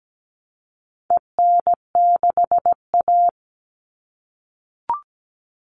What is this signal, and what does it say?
Morse code, heard as a keyed tone, about 17 wpm, 700 Hz. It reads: EN6A
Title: All Cut Numbers - 4-Digit 17wpm